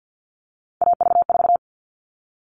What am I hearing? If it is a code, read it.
Morse code, heard as a keyed tone, about 50 wpm, 700 Hz. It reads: AV4
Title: All Cut Numbers - 3-Digit 50wpm